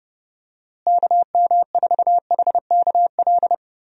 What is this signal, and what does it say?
Morse code, heard as a keyed tone, about 32 wpm, 700 Hz. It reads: KM4HKL
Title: USA Call Signs - 6 Characters - Repeated 2x - 30wpm